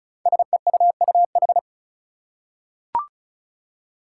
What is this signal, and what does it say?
Morse code, heard as a keyed tone, about 36 wpm, 690 Hz. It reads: SEUUH
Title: Code Groups - 5 Letters - 35wpm